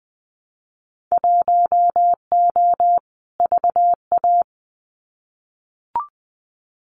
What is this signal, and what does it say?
Morse code, heard as a keyed tone, about 20 wpm, 700 Hz. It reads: 1O VA
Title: Winter Field Day Exchanges 20wpm